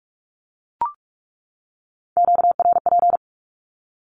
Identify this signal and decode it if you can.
Morse code, heard as a keyed tone, about 44 wpm, 700 Hz. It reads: QRP
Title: Q-Codes 45wpm